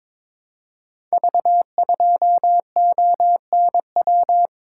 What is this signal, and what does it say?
Morse code, heard as a keyed tone, about 22 wpm, 700 Hz. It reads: V2ONW
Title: Code Groups - 5 Letter:Numbers - 22wpm